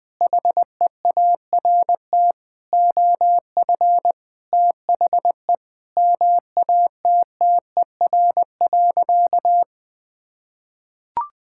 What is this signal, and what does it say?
Morse code, heard as a keyed tone, about 20 wpm, 700 Hz. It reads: HEART OF THE MATTER.
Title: Sentences with 4-Word Phrases - Spoken Beginning - 20wpm